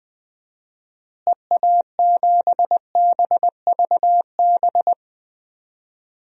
Morse code, heard as a keyed tone, about 20 wpm, 700 Hz. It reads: EA7BVB
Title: DXCC-100 Call Signs - 6 Characters - Repeated 2x - 20wpm